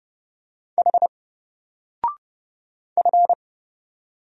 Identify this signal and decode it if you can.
Morse code, heard as a keyed tone, about 30 wpm, 700 Hz. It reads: H F
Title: Single Letter-Number Alt 30wpm